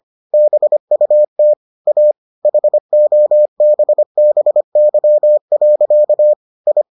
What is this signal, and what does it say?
Morse code, heard as a keyed tone, about 25 wpm, 600 Hz. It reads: BUT A HOBBY. I